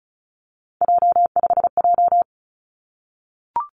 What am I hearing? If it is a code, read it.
Morse code, heard as a keyed tone, about 36 wpm, 700 Hz. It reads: J5J
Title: Three Challenging Letter-Number Combinations 35wpm